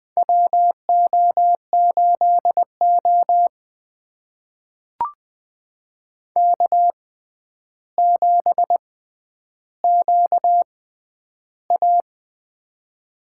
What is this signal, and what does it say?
Morse code, heard as a keyed tone, about 20 wpm, spaced as for 10 wpm, 700 Hz. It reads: WO8O K7QA
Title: USA Call Signs - 4 Characters - 6x Character Spacing - 20wpm